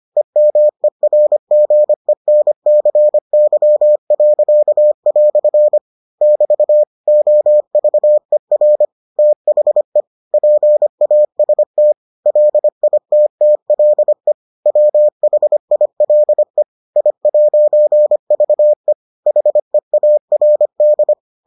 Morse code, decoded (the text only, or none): EMERGENCY." <BT> OVER THE PAST LITTLE WHILE I'VE HEARD